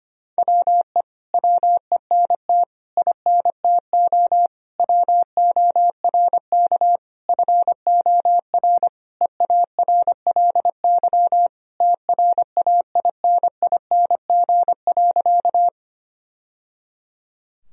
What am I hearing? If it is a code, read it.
Morse code, heard as a keyed tone, about 25 wpm, 700 Hz. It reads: WE WENT INTO WORK FOR EARLY TRAINING.
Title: Sentences from Top 1000 Words - Review Part 4 - 25wpm